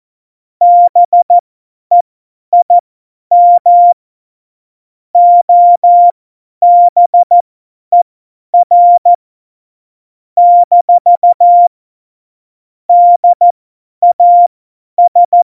Morse code, heard as a keyed tone, about 14 wpm, spaced as for 10 wpm, 700 Hz. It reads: BEIM OBER - DAS
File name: Morsekurs / Witze / Witz 03.Dezember 2019, 50 F70